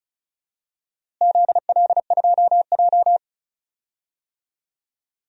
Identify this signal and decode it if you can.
Morse code, heard as a keyed tone, about 36 wpm, 690 Hz. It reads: ZL2J
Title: DXCC-100 Call Signs - 4 Characters - Repeated 3x - 35wpm